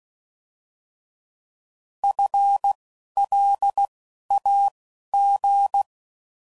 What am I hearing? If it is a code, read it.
Morse code, heard as a keyed tone, about 16 wpm, spaced as for 11 wpm, 800 Hz. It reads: FLAG
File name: Prefixes / By First Letter / Fs